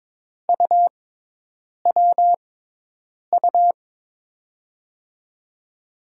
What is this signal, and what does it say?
Morse code, heard as a keyed tone, about 22 wpm, spaced as for 8 wpm, 700 Hz. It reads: UWU
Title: USA Call Signs - 5 Characters - 6x Character Spacing - 22wpm